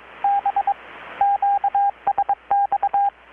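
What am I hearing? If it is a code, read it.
Morse code, heard as a keyed tone, about 21 wpm, 800 Hz, with radio noise. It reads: B QSX
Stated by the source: Morse from KPH's Night of Nights XX maritime radiotelegraphy event, received in Chiapas, Mexico about 3,800 km away on a Kenwood R-600 with a grounded 10 m long-wire antenna.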